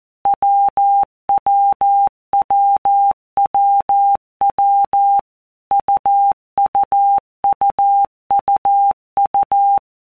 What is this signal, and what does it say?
Morse code, heard as a keyed tone, about 14 wpm, 800 Hz. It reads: WWWWW UUUUU
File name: Basic Letter Groups By Blocks / Groups@13 / AWUNDG@13